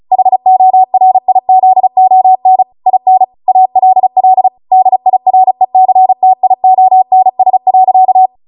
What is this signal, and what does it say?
Morse code, heard as a keyed tone, about 36 wpm, 750 Hz. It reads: HORIZON IN ALL DIRECTIONS.
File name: Morse / 35WPM